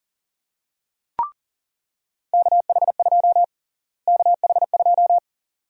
Morse code, heard as a keyed tone, about 40 wpm, 700 Hz. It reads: KH2 KH2
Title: USA Call Sign Prefixes - Repeated 2x - 40wpm